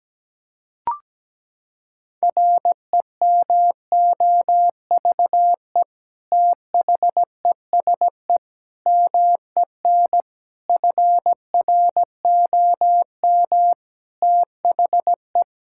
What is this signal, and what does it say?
Morse code, heard as a keyed tone, about 17 wpm, 700 Hz. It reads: REMOVE THESE MEN FROM THE
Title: Sentences from Top 1000 Words - Review Part 5 - 17wpm